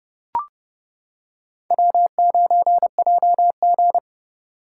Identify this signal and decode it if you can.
Morse code, heard as a keyed tone, about 28 wpm, 700 Hz. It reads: W9JG